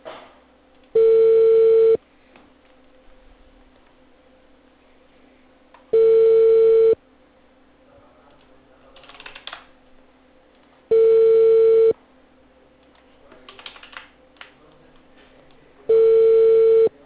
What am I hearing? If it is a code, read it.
Ringback tone (China, 450 Hz).